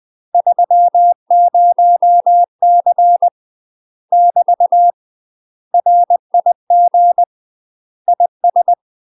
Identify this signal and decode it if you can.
Morse code, heard as a keyed tone, about 20 wpm, 700 Hz. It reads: 30C <BT> RIG IS
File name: UK MASTER QSOs TRAINING / 20WPM / UK MASTER QSOs 0043